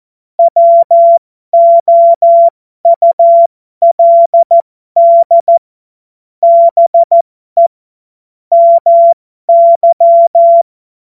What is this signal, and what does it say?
Morse code, heard as a keyed tone, about 14 wpm, spaced as for 12 wpm, 680 Hz. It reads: WOULD BE MY